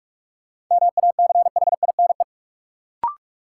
Morse code, heard as a keyed tone, about 46 wpm, 700 Hz. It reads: MACHINE
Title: Sentences from Top 1000 Words - Review Part 2 - New Word Drill - 45wpm